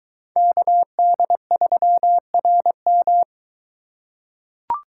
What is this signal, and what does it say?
Morse code, heard as a keyed tone, about 22 wpm, 700 Hz. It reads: KD3RM